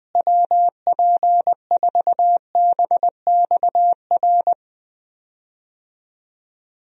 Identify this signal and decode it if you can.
Morse code, heard as a keyed tone, about 20 wpm, 700 Hz. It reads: WP4BXR
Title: USA Call Signs - 6 Characters - Repeated 2x - 20wpm